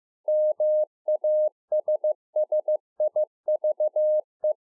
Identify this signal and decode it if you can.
Morse code, heard as a keyed tone, about 15 wpm, 610 Hz. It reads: MASSIVE
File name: 2011-04-mp3 / THE 3.11 6 15WPM